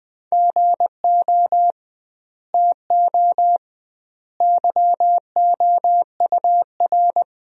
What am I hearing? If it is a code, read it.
Morse code, heard as a keyed tone, about 21 wpm, 700 Hz. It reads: GO TO YOUR
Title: 2x Word-Spacing - Sentences from Top 300 Words - 20wpm